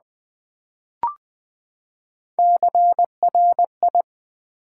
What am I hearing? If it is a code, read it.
Morse code, heard as a keyed tone, about 20 wpm, 700 Hz. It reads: CRI
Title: Build A Word - 3 Letters At A Time - 8-Letter Words - 20wpm